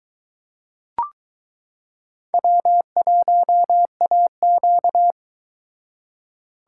Morse code, heard as a keyed tone, about 22 wpm, 700 Hz. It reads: W1AQ